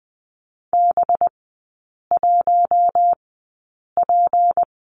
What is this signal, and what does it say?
Morse code, heard as a keyed tone, about 20 wpm, spaced as for 8 wpm, 700 Hz. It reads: B1P
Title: 191 Lesson 31e - Letter v - Three Characters 20wpm